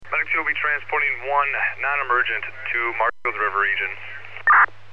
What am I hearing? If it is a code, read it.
Emergency Alert System (SAME) tones, heard as harsh buzzing bursts, with speech. It is heard from 0.2 to 4.3 s.